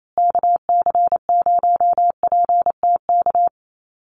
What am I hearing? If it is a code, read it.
Morse code, heard as a keyed tone, about 28 wpm, 700 Hz. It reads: KC0PTK
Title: USA Call Signs Repeated 2x 28wpm